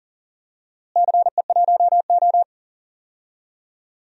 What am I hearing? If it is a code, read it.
Morse code, heard as a keyed tone, about 37 wpm, 700 Hz. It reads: CE1O
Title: DXCC-100 Call Signs - 4 Characters - Repeated 3x - 40wpm